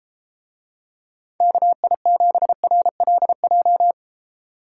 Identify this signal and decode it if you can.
Morse code, heard as a keyed tone, about 33 wpm, 700 Hz. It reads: KI7RLJ